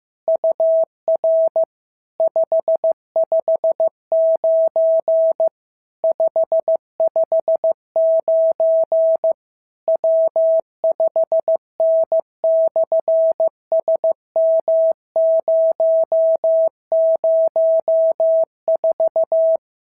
Morse code, heard as a keyed tone, about 15 wpm, 650 Hz. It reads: UR 559 559 W5N/SM004
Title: SOTA QSO Practice 15wpm